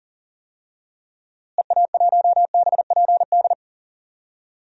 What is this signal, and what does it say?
Morse code, heard as a keyed tone, about 40 wpm, 700 Hz. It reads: EA1BPD
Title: DXCC-100 Call Signs - 6 Characters - Repeated 2x - 40wpm